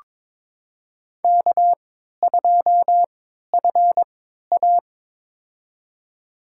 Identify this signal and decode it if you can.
Morse code, heard as a keyed tone, about 22 wpm, spaced as for 13 wpm, 690 Hz. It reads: K2FA
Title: USA Call Signs - 4 Characters - 3x Character Spacing - 22wpm